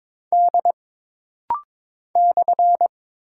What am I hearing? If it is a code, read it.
Morse code, heard as a keyed tone, about 22 wpm, 700 Hz. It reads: D /
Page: This rapid-fire timing is geared to push your ICR (Instant Character Recognition) skills.